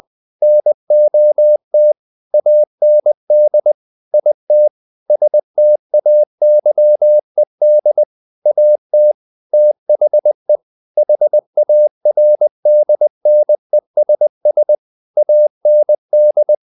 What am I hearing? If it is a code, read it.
Morse code, heard as a keyed tone, about 20 wpm, 600 Hz. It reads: NOT AND IT STAYED AT THE HARDNESS AND
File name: the old man and the sea part 2 20wpm